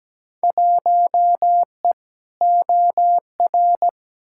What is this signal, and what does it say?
Morse code, heard as a keyed tone, about 17 wpm, 700 Hz. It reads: 1E OR
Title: Field Day Exchanges Encore - 17wpm